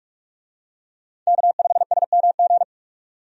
Morse code, heard as a keyed tone, about 45 wpm, 700 Hz. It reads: K5SMG
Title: USA Call Signs - 5 Characters - 45wpm